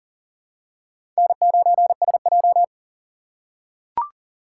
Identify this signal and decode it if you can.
Morse code, heard as a keyed tone, about 40 wpm, 700 Hz. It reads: N9SJ
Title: USA Call Signs - 3x Character Spacing - 40wpm